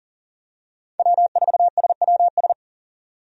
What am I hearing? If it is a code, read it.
Morse code, heard as a keyed tone, about 40 wpm, 700 Hz. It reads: W4SWS
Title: USA Call Signs - 5 Characters - 40wpm